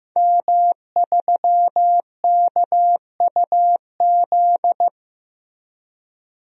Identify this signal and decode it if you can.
Morse code, heard as a keyed tone, about 15 wpm, 700 Hz. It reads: M3KUZ
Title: DXCC-100 Call Signs - 5 Characters - Repeated 2x - 15wpm